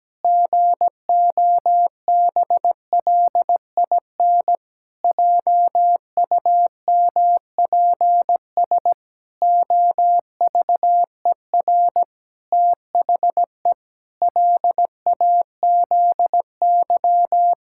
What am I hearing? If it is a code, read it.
Morse code, heard as a keyed tone, about 17 wpm, 700 Hz. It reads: GOBLIN JUMPS OVER THE LAZY
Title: Pangram Challenge v2 - 17wpm